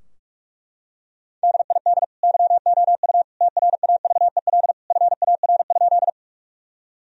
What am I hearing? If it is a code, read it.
Morse code, heard as a keyed tone, about 42 wpm, 700 Hz. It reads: DID YOU TRAVEL FAR?
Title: Sentences from Top 1000 Words - Review Part 2 - 45wpm